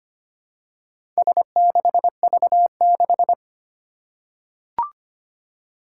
Morse code, heard as a keyed tone, about 25 wpm, 700 Hz. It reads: S6V6
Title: Four Challenging Letter-Number Combinations 25wpm